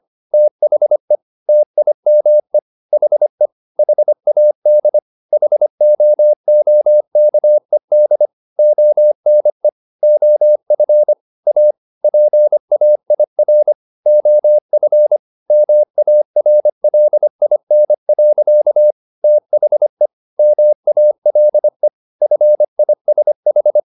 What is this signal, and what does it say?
Morse code, heard as a keyed tone, about 25 wpm, 600 Hz. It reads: THE TIME HE HAD HOOKED ONE OF A PAIR OF MARLIN. THE MALE FISH
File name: the old man and the sea part 1 25wpm